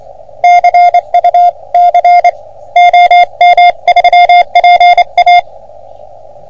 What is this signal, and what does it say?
Morse code, heard as a keyed tone, about 26 wpm, 710 Hz. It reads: CUC OM3PA
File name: OM3PA PILUP CUC 388